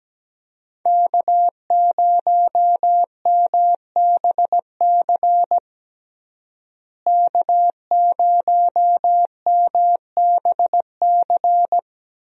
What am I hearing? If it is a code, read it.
Morse code, heard as a keyed tone, about 17 wpm, 700 Hz. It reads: K0MBC K0MBC
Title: USA Call Signs - 5 Characters - Repeated 3x - 17wpm